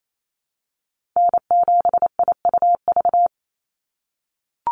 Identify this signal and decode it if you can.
Morse code, heard as a keyed tone, about 28 wpm, 700 Hz. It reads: N7IUV
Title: USA Call Signs 28wpm